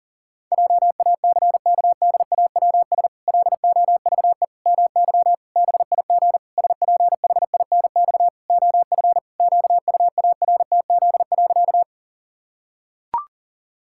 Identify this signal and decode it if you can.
Morse code, heard as a keyed tone, about 40 wpm, 700 Hz. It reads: JACKDAWS LOVE MY BIG SPHINX OF QUARTZ.
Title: Pangram Challenge v2 - 40wpm